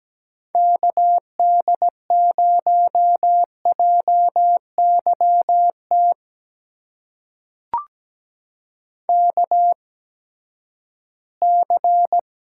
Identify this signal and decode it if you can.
Morse code, heard as a keyed tone, about 17 wpm, spaced as for 12 wpm, 700 Hz. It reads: KD0JYT KC
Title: USA Call Signs - 6 Characters - 8x Character Spacing - 17wpm